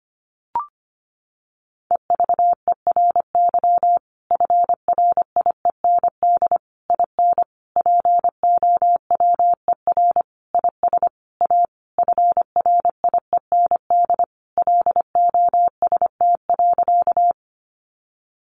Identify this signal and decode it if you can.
Morse code, heard as a keyed tone, about 25 wpm, 700 Hz. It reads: EVERY FRIEND IN POWER IS A FRIEND LOST.
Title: Sentences from Top 1400 Words 25wpm